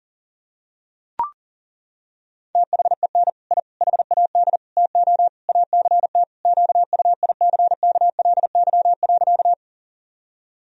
Morse code, heard as a keyed tone, about 37 wpm, 700 Hz. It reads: THEN I HAD TO ACT QUICKLY.
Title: Sentences from Top 1000 Words - Review Part 1 - 40wpm